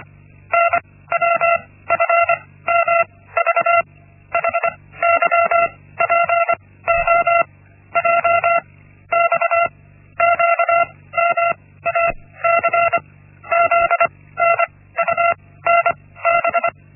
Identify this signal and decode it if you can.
Morse code, heard as a keyed tone, about 25 wpm, spaced as for 16 wpm, 680 Hz. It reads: NWFMV HYPOJ XQMAC ZNUNB
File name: Linksmakalnis-1